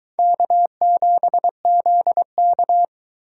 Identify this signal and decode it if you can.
Morse code, heard as a keyed tone, about 22 wpm, 700 Hz. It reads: K7ZK